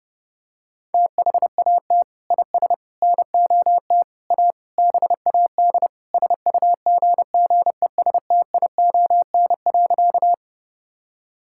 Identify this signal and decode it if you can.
Morse code, heard as a keyed tone, about 32 wpm, 700 Hz. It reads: THAT IS NOT A BAD SUGGESTION.
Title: Sentences from Top 1600 Words 30wpm